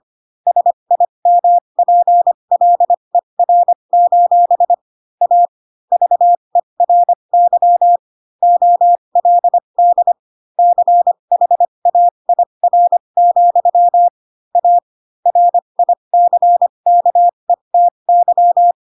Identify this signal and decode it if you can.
Morse code, heard as a keyed tone, about 25 wpm, spaced as for 22 wpm, 700 Hz. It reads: SIMPLER: A VERY OLD CHAIR, A RICKETY
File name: Pinocchio / Pinocchio ch3 18 25 ik1hge